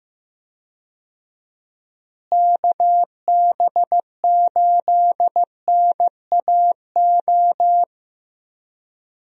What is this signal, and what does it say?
Morse code, heard as a keyed tone, about 15 wpm, 700 Hz. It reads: KB8NAO
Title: USA Call Signs - 6 Characters - Repeated 2x - 15wpm